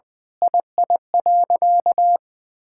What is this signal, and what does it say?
Morse code, heard as a keyed tone, about 20 wpm, 700 Hz. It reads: II.
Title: Sentences with 7-Word Phrases - Spoken Beginning - 20wpm